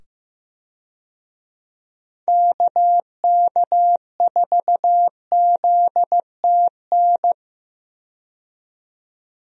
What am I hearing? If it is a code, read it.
Morse code, heard as a keyed tone, about 16 wpm, 700 Hz. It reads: KK4ZTN
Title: USA Call Signs - 6 Characters - 15wpm